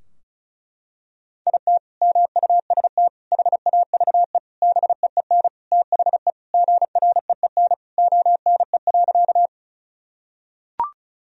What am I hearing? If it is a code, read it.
Morse code, heard as a keyed tone, about 34 wpm, 700 Hz. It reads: IT MUST HAVE BEEN THE GREEN ONE.
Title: Sentences from Top 1000 Words - Review Part 2 - 35wpm